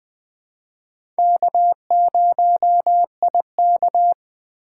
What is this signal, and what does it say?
Morse code, heard as a keyed tone, about 21 wpm, 700 Hz. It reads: K0IK
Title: USA Call Signs - 4 Characters - 2x Character Spacing - 20wpm